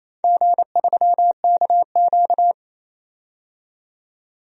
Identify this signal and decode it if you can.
Morse code, heard as a keyed tone, about 29 wpm, 700 Hz. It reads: G3KQ
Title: DXCC-100 Call Signs - 4 Characters - 28wpm